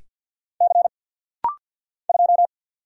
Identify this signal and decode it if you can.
Morse code, heard as a keyed tone, about 50 wpm, 690 Hz. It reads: C 2
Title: Single Letter-Number Alt - Rapid-Fire 50wpm